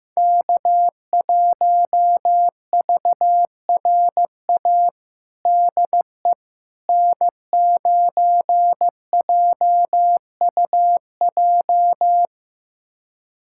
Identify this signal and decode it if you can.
Morse code, heard as a keyed tone, about 15 wpm, 700 Hz. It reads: K1VRA DE N9JUJ
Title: USA Callsign DE Callsign - 5 Characters - 15wpm